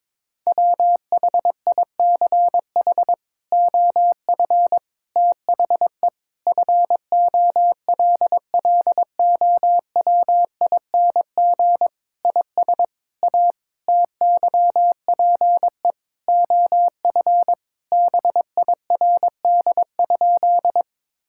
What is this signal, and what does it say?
Morse code, heard as a keyed tone, about 22 wpm, 700 Hz. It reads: WHICH OF THE FOLLOWING IS A TYPE OF BIRD?
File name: practice / 22wpm / Sentences with 8-Word Phrases - 22wpm